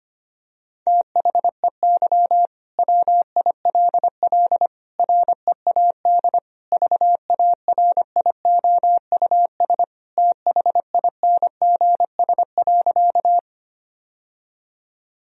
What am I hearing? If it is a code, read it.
Morse code, heard as a keyed tone, about 25 wpm, 700 Hz. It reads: THEY WILL READ VARIOUS THINGS.
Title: Sentences from Top 1000 Words - Review Part 4 - 25wpm